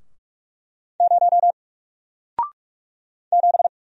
Morse code, heard as a keyed tone, about 44 wpm, 700 Hz. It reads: 0 7
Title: Single Character (Letter:Number:Punctuation:Prosign) Alt 45wpm